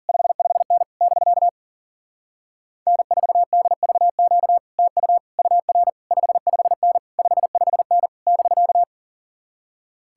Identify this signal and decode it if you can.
Morse code, heard as a keyed tone, about 52 wpm, 700 Hz. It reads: 55N <BK> N4DVQ TU UR 55N 55N <BK>